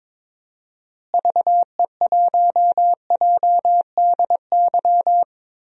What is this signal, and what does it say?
Morse code, heard as a keyed tone, about 22 wpm, 700 Hz. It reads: VE1JDY
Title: DXCC-100 Call Signs - 6 Characters - Repeated 2x - 22wpm